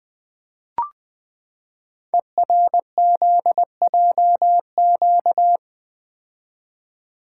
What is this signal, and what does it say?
Morse code, heard as a keyed tone, about 20 wpm, 700 Hz. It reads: ERZJQ
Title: Code Groups - 5 Letters - 20wpm